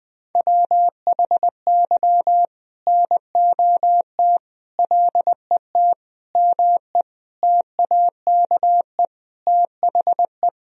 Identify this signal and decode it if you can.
Morse code, heard as a keyed tone, about 21 wpm, 700 Hz. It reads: WHY NOT LET ME TAKE THE
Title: Sentences from Top 1000 Words - Review Part 2 - 20wpm